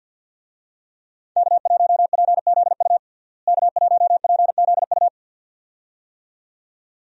Morse code, heard as a keyed tone, about 53 wpm, 700 Hz. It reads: K1PZU K1PZU
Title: USA Call Signs - 5 Characters - Repeated 2x - 50wpm